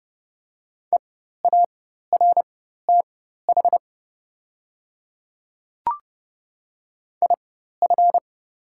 Morse code, heard as a keyed tone, about 32 wpm, spaced as for 14 wpm, 690 Hz. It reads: EARTH IF
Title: 4x Character Spacing - Top 200 Words - 30wpm